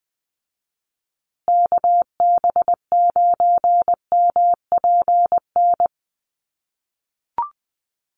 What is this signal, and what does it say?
Morse code, heard as a keyed tone, about 20 wpm, 700 Hz. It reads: KB9MPN
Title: USA Call Signs Repeated 3x 20wpm